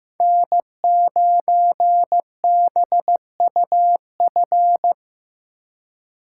Morse code, heard as a keyed tone, about 18 wpm, 700 Hz. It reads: N9BUF
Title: Speed Racing - USA 1x3 Call Signs 15wpm